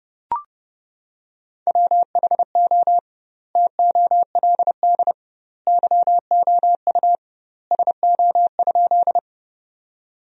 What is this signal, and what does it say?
Morse code, heard as a keyed tone, about 32 wpm, 700 Hz. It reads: WHO TOLD YOU SO?
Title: 2x Word-Spacing - Sentences from Top 400 Words - 30wpm